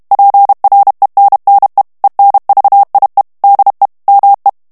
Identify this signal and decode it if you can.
Morse code, heard as a keyed tone, about 32 wpm, 800 Hz. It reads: PRENNE ENVIE DE ME
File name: FRANCAIS TEXTE CLAIR 2 VITESSE 30 MOTS MINUTES